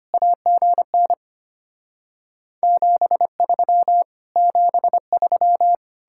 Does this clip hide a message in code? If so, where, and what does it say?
Yes: Morse code from 0.1 to 5.8 s, heard as a keyed tone, about 33 wpm, 700 Hz. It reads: AGN 73 73